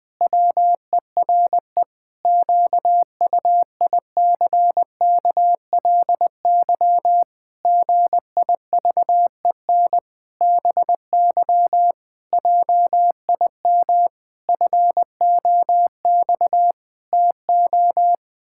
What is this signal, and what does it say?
Morse code, heard as a keyed tone, about 20 wpm, 700 Hz. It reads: WERE QUICKLY GIVEN BY JIM FOX TO
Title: Pangram Challenge v2 - 20wpm